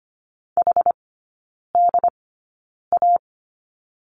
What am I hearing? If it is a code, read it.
Morse code, heard as a keyed tone, about 25 wpm, spaced as for 9 wpm, 700 Hz. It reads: HDA
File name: Lesson 15e - Number 5 - Three Characters 25wpm